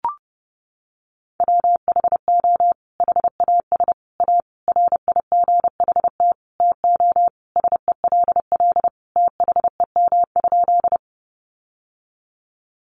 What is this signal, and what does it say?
Morse code, heard as a keyed tone, about 30 wpm, 700 Hz. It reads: WHO HAS A RIGHT TO SELL THEM?
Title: Sentences from Top 1100 Words 30wpm